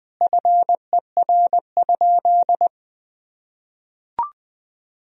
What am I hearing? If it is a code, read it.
Morse code, heard as a keyed tone, about 21 wpm, 700 Hz. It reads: FER?
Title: Sentences from Top 100 Words and Abbreviations 20wpm